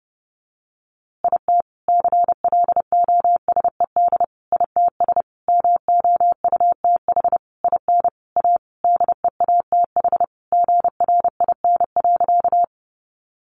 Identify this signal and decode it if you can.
Morse code, heard as a keyed tone, about 30 wpm, 700 Hz. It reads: IT CLOSED ITS MOUTH IN A DEATH GRIN.
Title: A Sound Of Thunder 30wpm